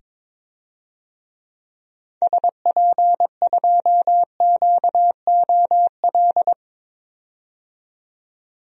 Morse code, heard as a keyed tone, about 22 wpm, 700 Hz. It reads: SP2QOL
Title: DXCC-100 Call Signs - 6 Characters - 22wpm